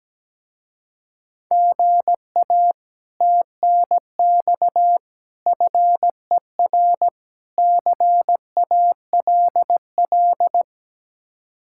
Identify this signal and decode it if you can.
Morse code, heard as a keyed tone, about 17 wpm, 700 Hz. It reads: GA TNX FER CALL
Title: Speed-Racing Common QSO Phrases - 17wpm